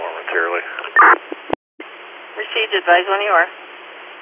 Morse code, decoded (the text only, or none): none